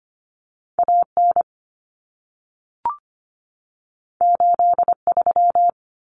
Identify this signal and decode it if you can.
Morse code, heard as a keyed tone, about 25 wpm, 700 Hz. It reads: AN 83
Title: Common Cut Numbers - 2-Digit 25wpm